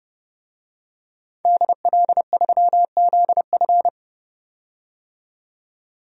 Morse code, heard as a keyed tone, about 30 wpm, 700 Hz. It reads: DL3ZF
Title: DXCC-100 Call Signs - 5 Characters - 30wpm